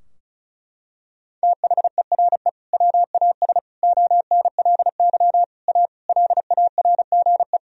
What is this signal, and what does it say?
Morse code, heard as a keyed tone, about 36 wpm, 700 Hz. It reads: THERE WAS ONLY A LARGE
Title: Sentences from Top 1000 Words - Review Part 2 - 35wpm